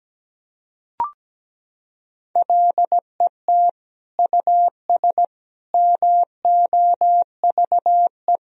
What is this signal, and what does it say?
Morse code, heard as a keyed tone, about 17 wpm, 700 Hz. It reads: LET US MOVE
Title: Sentences from Top 1000 Words - Review Part 2 - 17wpm